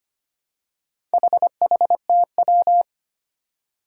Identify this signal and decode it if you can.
Morse code, heard as a keyed tone, about 25 wpm, 700 Hz. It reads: HHTW
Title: Four Challenging Letter-Number Combinations 25wpm